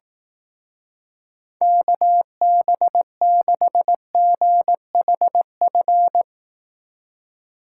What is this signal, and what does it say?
Morse code, heard as a keyed tone, about 19 wpm, 700 Hz. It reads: KB6GHF
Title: Speed Racing - USA 2x3 Call Signs 15wpm